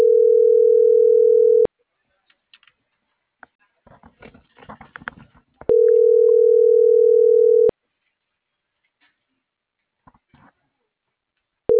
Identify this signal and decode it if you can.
Ringback tone (North America), 440 + 480 Hz.
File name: CENTRAL NMP / out isabelponciano